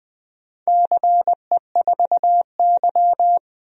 Morse code, heard as a keyed tone, about 20 wpm, 700 Hz. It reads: CE4Y
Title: DXCC-100 Call Signs - 4 Characters - 20wpm